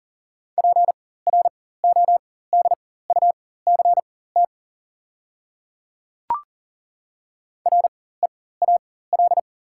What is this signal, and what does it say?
Morse code, heard as a keyed tone, about 40 wpm, spaced as for 19 wpm, 700 Hz. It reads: PRODUCT REAL
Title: 4x Character Spacing - Top 500 Words - 40wpm